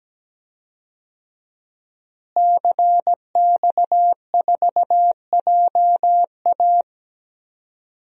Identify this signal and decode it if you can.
Morse code, heard as a keyed tone, about 17 wpm, 700 Hz. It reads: CX4JA
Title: DXCC-100 Call Signs - 5 Characters - Repeated 2x - 17wpm